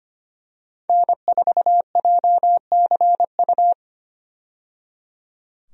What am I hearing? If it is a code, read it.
Morse code, heard as a keyed tone, about 25 wpm, 700 Hz. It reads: N4JCU
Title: USA Call Signs - 5 Characters - 25wpm